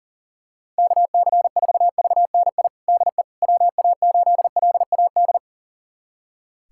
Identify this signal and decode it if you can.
Morse code, heard as a keyed tone, about 37 wpm, 700 Hz. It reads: KC4VNI DE WA8LAD
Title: USA Callsign DE Callsign 40wpm